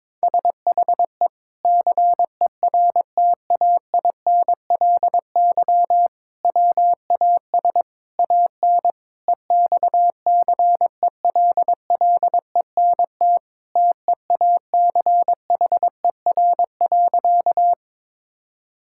Morse code, heard as a keyed tone, about 22 wpm, 700 Hz. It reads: SHE CERTAINLY WAS AN EXCELLENT TEACHER.
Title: Sentences from Top 1000 Words - Review Part 6 - 22wpm